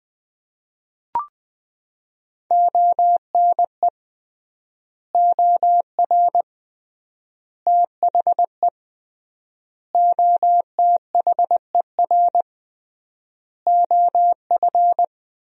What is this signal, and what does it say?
Morse code, heard as a keyed tone, about 21 wpm, 700 Hz. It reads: ONE OR THE OTHER OF
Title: 3x Word-Spacing - Sentences from Top 400 Words - 20wpm